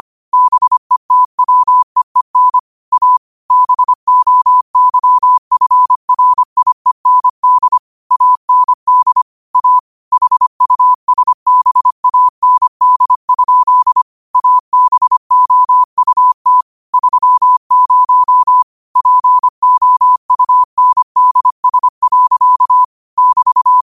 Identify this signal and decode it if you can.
Morse code, heard as a keyed tone, about 25 wpm, 1000 Hz. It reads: BETWEEN A BOYFRIEND AND A HUSBAND? ABOUT 30 POUNDS. <BT>